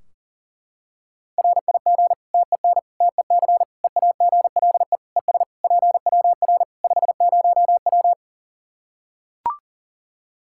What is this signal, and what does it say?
Morse code, heard as a keyed tone, about 40 wpm, 700 Hz. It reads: RIG TEN TEC EAGLE ES PWR 50W
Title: Practice QSO - Exchange 2 - Info - 40wpm